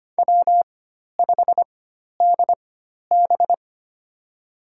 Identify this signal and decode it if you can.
Morse code, heard as a keyed tone, about 25 wpm, spaced as for 12 wpm, 700 Hz. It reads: W5DB
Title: USA Call Signs - 4 Characters - 4x Character Spacing - 25wpm